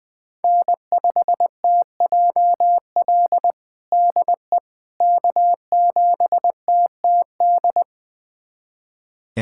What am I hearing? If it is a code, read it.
Morse code, heard as a keyed tone, about 20 wpm, 700 Hz. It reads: N5TJL DE K7TTD
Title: USA Callsign DE Callsign - 5 Characters - 20wpm